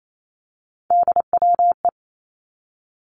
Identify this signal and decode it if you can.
Morse code, heard as a keyed tone, about 28 wpm, 700 Hz. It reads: DWE
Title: Three Challenging Letter-Number Combinations 28wpm